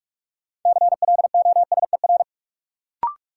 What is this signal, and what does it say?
Morse code, heard as a keyed tone, about 46 wpm, 700 Hz. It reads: CLOSER
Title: Sets of 5 Words Encore - New Word Drill - 45wpm